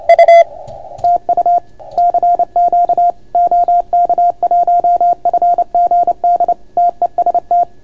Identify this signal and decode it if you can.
Morse code, heard as a keyed tone, about 29 wpm, 690 Hz. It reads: U TU CQ OK1FGD TEST